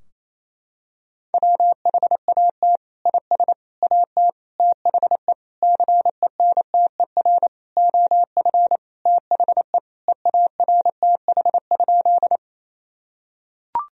Morse code, heard as a keyed tone, about 28 wpm, 700 Hz. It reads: WHAT IS AT THE CENTER OF THE EARTH?
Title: Sentences from Top 1000 Words - Review Part 2 - 28wpm